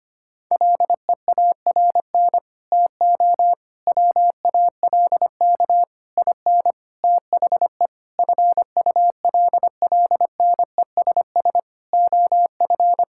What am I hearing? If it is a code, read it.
Morse code, heard as a keyed tone, about 25 wpm, 700 Hz. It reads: LEARN TO WALK IN THE FULLNESS OF
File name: Sets of 9 Words - 25wpm